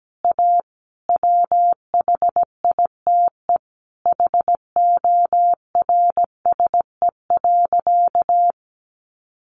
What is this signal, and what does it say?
Morse code, heard as a keyed tone, about 17 wpm, 700 Hz. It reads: A WHITE HORSE.
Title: Sentences from Top 300 Words 17wpm